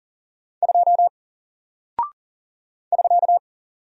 Morse code, heard as a keyed tone, about 40 wpm, 700 Hz. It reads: 2 <SK>
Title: Single Character (Letter:Number:Punctuation:Prosign) Alt 40wpm